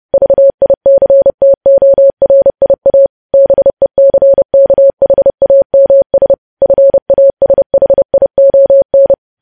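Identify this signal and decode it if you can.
Morse code, heard as a keyed tone, about 30 wpm, 550 Hz. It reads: VICTORIA BECKHAMS FASHION
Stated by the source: Morse code practice of news headlines at 30 wpm, preceded by vvv.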